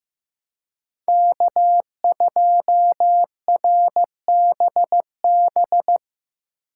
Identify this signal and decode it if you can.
Morse code, heard as a keyed tone, about 18 wpm, 700 Hz. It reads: K2RBB
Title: Speed Racing - USA 1x3 Call Signs 15wpm